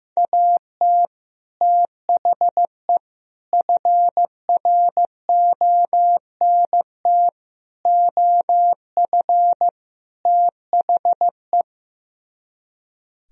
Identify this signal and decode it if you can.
Morse code, heard as a keyed tone, about 16 wpm, 700 Hz. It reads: AT THE FRONT OF THE
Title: Speed-Racing - Sets of 5 Words 15wpm